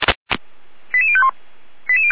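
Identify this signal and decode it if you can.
Five-tone selective call (ZVEI), heard as quick runs of five tones, tones of about 70 ms. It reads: 88051
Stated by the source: ZVEI five-tone selective call received by SDR.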